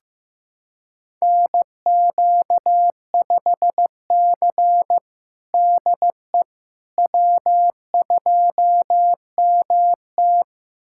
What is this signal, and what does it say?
Morse code, heard as a keyed tone, about 15 wpm, 700 Hz. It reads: NQ5C DE W2MT
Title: USA Callsign DE Callsign - 4 Characters - 15wpm